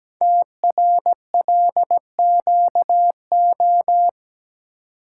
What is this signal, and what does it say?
Morse code, heard as a keyed tone, about 17 wpm, 700 Hz. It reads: TRLQO
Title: Code Groups - 5 Letter:Numbers - 17wpm